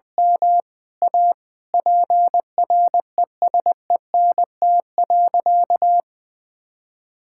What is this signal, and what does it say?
Morse code, heard as a keyed tone, about 21 wpm, 700 Hz. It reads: M A PRESENT.
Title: Sentences from Top 1000 Words - Review Part 5 - 20wpm